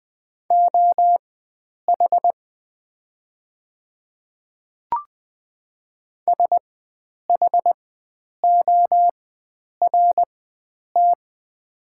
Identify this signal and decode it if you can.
Morse code, heard as a keyed tone, about 20 wpm, spaced as for 9 wpm, 700 Hz. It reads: OH SHORT
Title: 4x Character Spacing - Top 500 Words - 20wpm